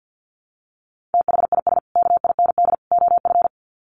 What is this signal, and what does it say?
Morse code, heard as a keyed tone, about 50 wpm, 700 Hz. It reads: THIS KIND OF
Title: Sets of 3 Words 50wpm